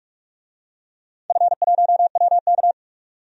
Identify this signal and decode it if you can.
Morse code, heard as a keyed tone, about 44 wpm, 700 Hz. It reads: F1WK
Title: DXCC-100 Call Signs - 4 Characters - 45wpm